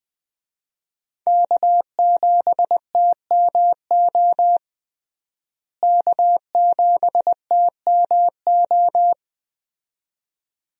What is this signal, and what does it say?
Morse code, heard as a keyed tone, about 20 wpm, 700 Hz. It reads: K7TMO K7TMO
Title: USA Call Signs - 5 Characters - Repeated 2x - 20wpm